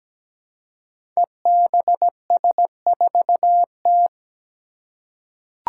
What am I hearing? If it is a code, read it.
Morse code, heard as a keyed tone, about 17 wpm, 700 Hz. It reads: EBS4T
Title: Five Challenging Letter-Number Combinations 17wpm